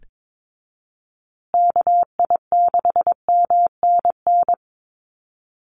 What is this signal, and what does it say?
Morse code, heard as a keyed tone, about 22 wpm, 700 Hz. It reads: KI6MNN
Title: USA Call Signs Repeated 2x 22wpm